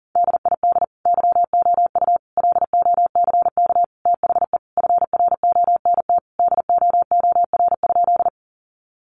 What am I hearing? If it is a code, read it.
Morse code, heard as a keyed tone, about 40 wpm, 700 Hz. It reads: DID YOU LOCK THE FRONT DOOR?
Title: Sentences from Top 1300 Words 40wpm